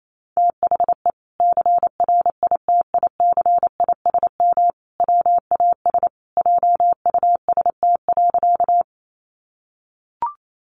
Morse code, heard as a keyed tone, about 28 wpm, 700 Hz. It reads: THE CRITICISM WAS JUST.
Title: Sentences from Top 1300 Words 28wpm